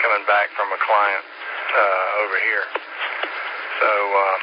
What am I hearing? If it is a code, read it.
Emergency Alert System (SAME) tones, heard as harsh buzzing bursts, with speech. It is heard from 1.2 to 3.7 s.